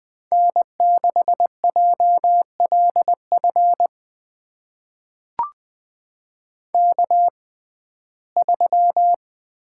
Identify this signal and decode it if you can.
Morse code, heard as a keyed tone, about 20 wpm, spaced as for 15 wpm, 700 Hz. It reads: N6JLF K3
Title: USA Call Signs - 5 Characters - 6x Character Spacing - 20wpm